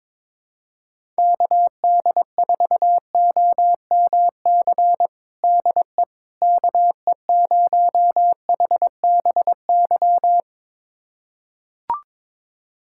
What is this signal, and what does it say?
Morse code, heard as a keyed tone, about 22 wpm, 700 Hz. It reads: KD4OMC DE KE0HBY
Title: USA Callsign DE Callsign - 6 Characters - 22wpm